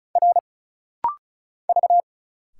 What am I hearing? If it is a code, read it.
Morse code, heard as a keyed tone, about 36 wpm, 700 Hz. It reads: R V
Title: Single Letters - Rapid-Fire 35wpm